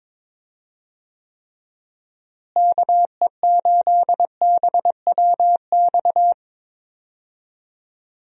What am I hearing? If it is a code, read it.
Morse code, heard as a keyed tone, about 22 wpm, 700 Hz. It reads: KE8BWX
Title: USA Call Signs - 6 Characters - Repeated 3x - 22wpm